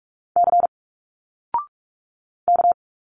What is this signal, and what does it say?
Morse code, heard as a keyed tone, about 46 wpm, 700 Hz. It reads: C K
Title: Single Letters 45wpm